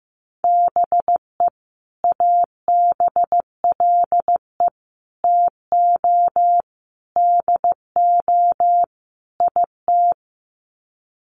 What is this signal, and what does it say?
Morse code, heard as a keyed tone, about 15 wpm, 700 Hz. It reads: BE ABLE TO DO IT
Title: Sets of 5 Words Encore2 15wpm